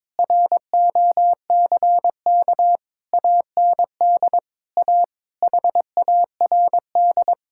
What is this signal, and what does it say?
Morse code, heard as a keyed tone, about 22 wpm, 700 Hz. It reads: ROCK AND A HARD
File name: practice / 22wpm / Sets of 7 Words - 22wpm